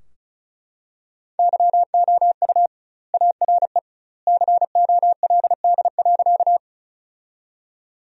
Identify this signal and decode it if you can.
Morse code, heard as a keyed tone, about 34 wpm, 700 Hz. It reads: YOU ARE COLD.
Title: 2x Word-Spacing - Sentences from Top 400 Words - 35wpm